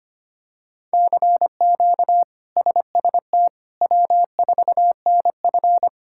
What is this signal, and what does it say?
Morse code, heard as a keyed tone, about 25 wpm, 700 Hz. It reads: CQ SST W4NF
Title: K1USN SST QSO Practice - 25wpm